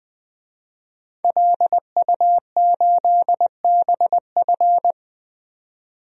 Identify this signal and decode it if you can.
Morse code, heard as a keyed tone, about 20 wpm, 700 Hz. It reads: LU8BF
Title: DXCC-100 Call Signs - 5 Characters - 20wpm